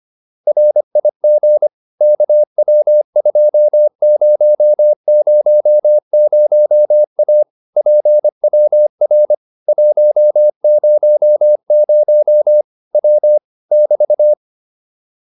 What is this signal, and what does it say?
Morse code, heard as a keyed tone, about 25 wpm, 600 Hz. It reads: RIG KW2000A PWR 100 W <BT>
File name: QSO 410 25